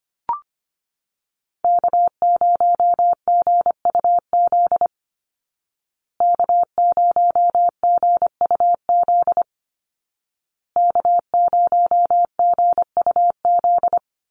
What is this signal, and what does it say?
Morse code, heard as a keyed tone, about 25 wpm, 700 Hz. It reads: K0GUZ K0GUZ K0GUZ
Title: Call Signs Repeated 2x - 25wpm